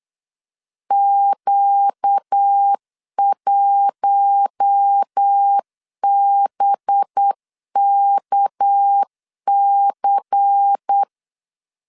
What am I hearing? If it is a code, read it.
Morse code, heard as a keyed tone, about 8 wpm, 800 Hz. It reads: Q1BKC
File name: LEZIONI / 6 PKX0 / 6 PK 8wpm 3003